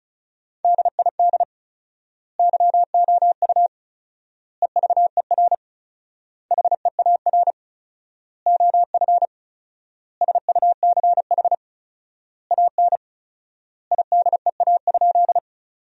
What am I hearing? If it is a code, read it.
Morse code, heard as a keyed tone, about 36 wpm, 700 Hz. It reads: DID YOU EVER HEAR OF SUCH AN IDEA?
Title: 4x Word-Spacing - Sentences from Top 400 Words - 35wpm